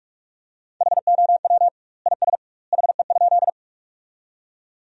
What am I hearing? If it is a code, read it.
Morse code, heard as a keyed tone, about 46 wpm, 700 Hz. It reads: HOW IS HE?
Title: 2x Word-Spacing - Sentences from Top 100 Words - 45wpm